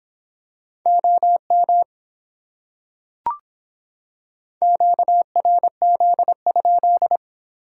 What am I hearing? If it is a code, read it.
Morse code, heard as a keyed tone, about 26 wpm, 700 Hz. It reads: OM QRZ?